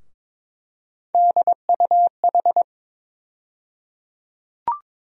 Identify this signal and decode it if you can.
Morse code, heard as a keyed tone, about 22 wpm, 700 Hz. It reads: DUH
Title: USA Call Sign Postfixes - 3 Letters 22wpm